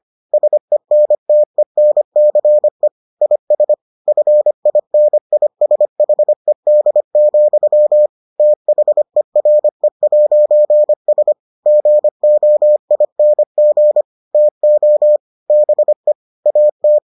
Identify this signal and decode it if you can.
Morse code, heard as a keyed tone, about 25 wpm, 600 Hz. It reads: SENTENCE IS FINISHED, THERE'S GOING TO BE AT